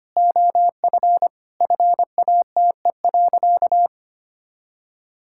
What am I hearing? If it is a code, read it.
Morse code, heard as a keyed tone, about 25 wpm, 700 Hz. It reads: OF FATE.
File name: practice / 25wpm / Sentences with 4-Word Phrases - 25wpm